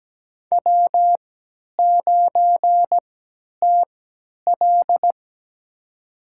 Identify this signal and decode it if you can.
Morse code, heard as a keyed tone, about 17 wpm, spaced as for 10 wpm, 700 Hz. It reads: W9TL
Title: USA Call Signs - 4 Characters - 3x Character Spacing - 17wpm